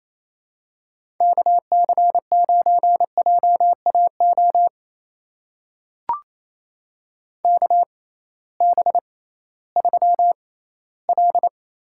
Morse code, heard as a keyed tone, about 28 wpm, spaced as for 16 wpm, 700 Hz. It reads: KC9JAO KB3L
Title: USA Call Signs - 6 Characters - 6x Character Spacing - 28wpm